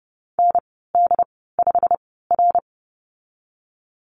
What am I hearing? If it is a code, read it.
Morse code, heard as a keyed tone, about 31 wpm, spaced as for 17 wpm, 700 Hz. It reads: ND5R
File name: Lesson 19g - Letter w - Call Signs 30wpm